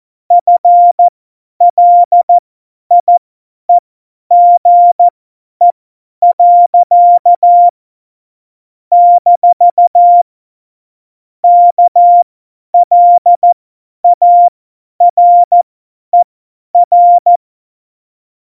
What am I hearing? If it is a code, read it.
Morse code, heard as a keyed tone, about 14 wpm, spaced as for 10 wpm, 700 Hz. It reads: FLIEGE. - KLARER
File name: Morsekurs / Witze / Witz 24. September 2019, 50 F70